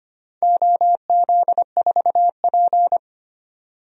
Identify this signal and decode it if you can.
Morse code, heard as a keyed tone, about 25 wpm, 700 Hz. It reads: OZ4P
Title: DXCC-100 Call Signs - 4 Characters - 25wpm